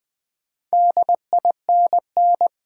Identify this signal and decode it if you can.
Morse code, heard as a keyed tone, about 20 wpm, 700 Hz. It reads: DINN
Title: Build A Word - 2 Letters At A Time - 6-Letter Words - 20wpm